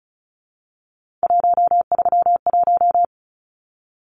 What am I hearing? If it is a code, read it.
Morse code, heard as a keyed tone, about 34 wpm, 700 Hz. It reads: 131
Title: Numbers - 3-Digit 35wpm